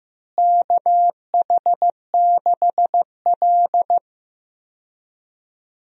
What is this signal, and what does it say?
Morse code, heard as a keyed tone, about 15 wpm, 700 Hz. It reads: KH6L
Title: DXCC-100 Call Signs - 4 Characters - Repeated 3x - 15wpm